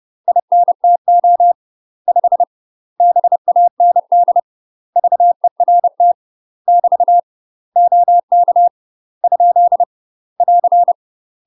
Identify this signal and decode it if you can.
Morse code, heard as a keyed tone, about 30 wpm, 700 Hz. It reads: INTO 5 BAND VERT <BT> OK ? <AR>
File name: UK MASTER QSOs TRAINING / 30WPM / UK MASTER QSOs 0031